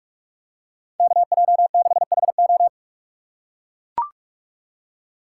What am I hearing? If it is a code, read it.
Morse code, heard as a keyed tone, about 40 wpm, 700 Hz. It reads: KJ6HO